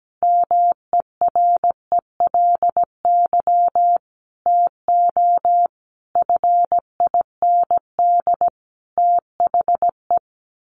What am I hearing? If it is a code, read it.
Morse code, heard as a keyed tone, about 17 wpm, 700 Hz. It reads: MERELY TO FIND THE
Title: Sentences from Top 1400 Words 17wpm